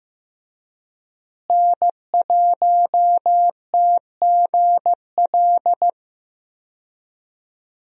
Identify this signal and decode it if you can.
Morse code, heard as a keyed tone, about 15 wpm, 700 Hz. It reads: N1TGL
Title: USA Call Signs - 5 Characters - Repeated 2x - 15wpm